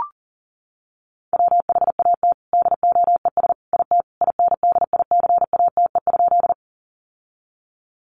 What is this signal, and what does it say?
Morse code, heard as a keyed tone, about 40 wpm, 700 Hz. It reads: WHAT DOES IT INDICATE?
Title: Sentences from Top 1300 Words 40wpm